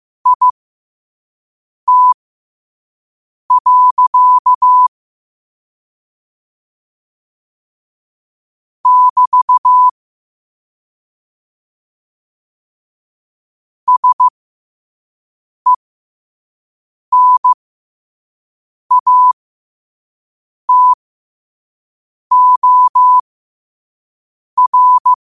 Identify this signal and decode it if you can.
Morse code, heard as a keyed tone, about 15 wpm, spaced as for 5 wpm, 1000 Hz. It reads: IT. <BT> SENATOR
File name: qotdfiles / 5wpm